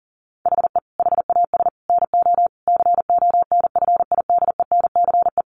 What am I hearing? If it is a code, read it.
Morse code, heard as a keyed tone, about 37 wpm, 700 Hz. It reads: HE HAS NO CONFIDENCE
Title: Sentences from Top 1500 Words 40wpm